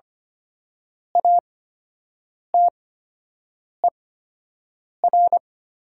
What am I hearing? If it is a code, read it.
Morse code, heard as a keyed tone, about 25 wpm, spaced as for 7 wpm, 700 Hz. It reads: ATER
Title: 8x Character Spacing - Top 300 Words - 25wpm